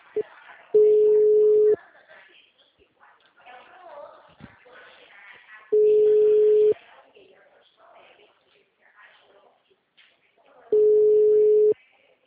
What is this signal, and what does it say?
Ringback tone (Europe, 425 Hz), with speech.
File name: out- -2006- - - .859